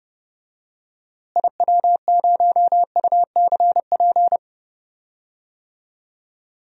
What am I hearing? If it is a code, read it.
Morse code, heard as a keyed tone, about 30 wpm, 700 Hz. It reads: IW0UCP
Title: DXCC-100 Call Signs - 6 Characters - Repeated 3x - 30wpm